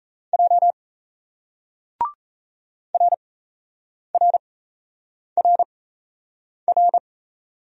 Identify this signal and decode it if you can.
Morse code, heard as a keyed tone, about 37 wpm, spaced as for 8 wpm, 700 Hz. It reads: J RRRR